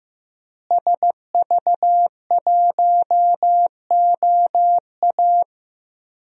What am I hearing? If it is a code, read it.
Morse code, heard as a keyed tone, about 15 wpm, 700 Hz. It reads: SV1OA
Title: DXCC-100 Call Signs - 5 Characters - Repeated 3x - 15wpm